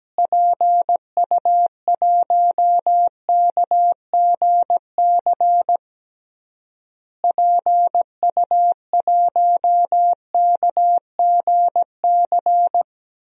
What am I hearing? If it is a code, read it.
Morse code, heard as a keyed tone, about 17 wpm, 700 Hz. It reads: PU1KGC PU1KGC
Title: DXCC-100 Call Signs - 6 Characters - Repeated 3x - 17wpm